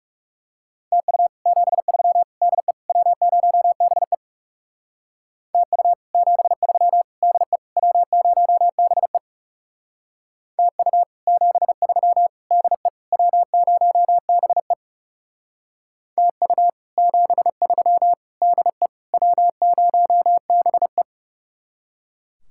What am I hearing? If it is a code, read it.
Morse code, heard as a keyed tone, about 39 wpm, 700 Hz. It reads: TU 73 DE W0BE TU 73 DE W0BE TU 73 DE W0BE TU 73 DE W0BE